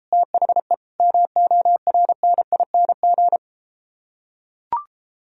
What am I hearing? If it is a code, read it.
Morse code, heard as a keyed tone, about 33 wpm, 700 Hz. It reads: THE MORNING